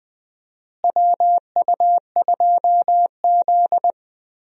Morse code, heard as a keyed tone, about 20 wpm, 700 Hz. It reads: WU2Z
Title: USA Call Signs - 4 Characters - Repeated 3x - 20wpm